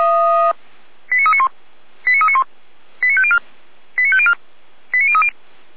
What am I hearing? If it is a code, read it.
Five-tone selective call (ZVEI), heard as quick runs of five tones, tones of about 70 ms. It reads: 89391 89391 89504 89504 89030
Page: ZVEI